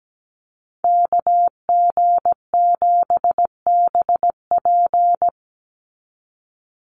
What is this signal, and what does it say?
Morse code, heard as a keyed tone, about 17 wpm, 700 Hz. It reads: KG7BP
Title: USA Call Signs 17wpm